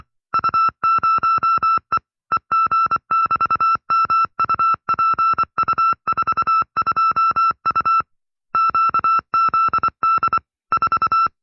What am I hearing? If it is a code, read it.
Morse code, heard as a keyed tone, about 24 wpm, 1330 Hz. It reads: U0E EG<BT>MUPU42U QZD 4
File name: FreeFiles / CW-practice / 20WPM-side2